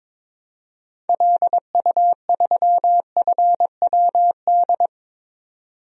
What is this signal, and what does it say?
Morse code, heard as a keyed tone, about 22 wpm, 700 Hz. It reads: LU3FWD
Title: DXCC-100 Call Signs - 6 Characters - 22wpm